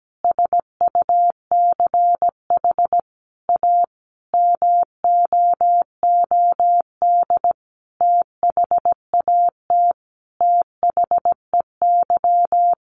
Morse code, heard as a keyed tone, about 17 wpm, 700 Hz. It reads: SUCH A MOOD THAT THEY
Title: Sentences from Top 1100 Words 17wpm